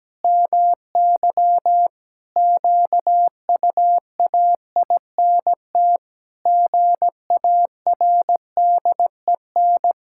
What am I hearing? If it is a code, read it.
Morse code, heard as a keyed tone, about 17 wpm, 700 Hz. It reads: MY QUAINT GARDEN
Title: Pangram Challenge v2 - 17wpm